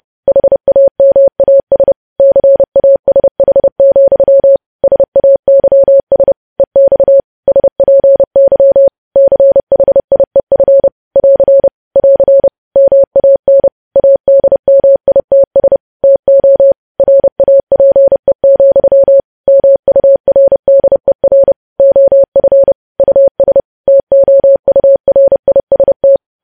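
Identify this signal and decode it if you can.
Morse code, heard as a keyed tone, about 30 wpm, 550 Hz. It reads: HAMAS CASH, SAYS EX SPY CHIEF <AR> <AR> MAN ADMITS TO RAPE, MURDER OF US TOURIST
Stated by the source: Morse code news headlines practice, preceded by vvv.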